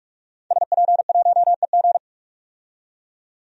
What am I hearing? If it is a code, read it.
Morse code, heard as a keyed tone, about 44 wpm, 690 Hz. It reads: SP1EG
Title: DXCC-100 Call Signs - 5 Characters - 45wpm